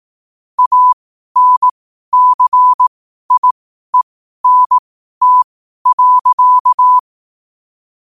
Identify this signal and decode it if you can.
Morse code, heard as a keyed tone, about 18 wpm, spaced as for 12 wpm, 1000 Hz. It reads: ANCIENT.
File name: qotdfiles / 12wpm